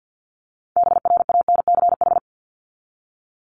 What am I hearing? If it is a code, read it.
Morse code, heard as a keyed tone, about 50 wpm, 700 Hz. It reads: BRANCH
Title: Top 1751-2000 Words 50wpm